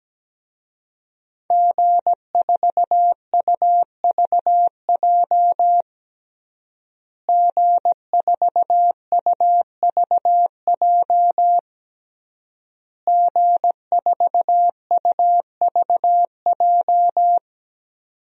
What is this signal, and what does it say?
Morse code, heard as a keyed tone, about 17 wpm, 700 Hz. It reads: G4UVJ G4UVJ G4UVJ
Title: DXCC-100 Call Signs - 5 Characters - Repeated 3x - 17wpm